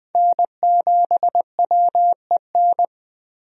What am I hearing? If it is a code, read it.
Morse code, heard as a keyed tone, about 20 wpm, 700 Hz. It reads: N7WEN
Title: USA Call Signs - 5 Characters - Repeated 3x - 20wpm